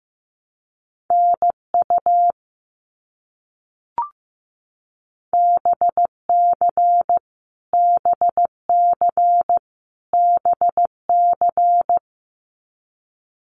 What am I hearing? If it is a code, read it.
Morse code, heard as a keyed tone, about 15 wpm, 700 Hz. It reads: NU BC BC BC
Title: Canadian Provinces and Territories - Repeated 3x - 15wpm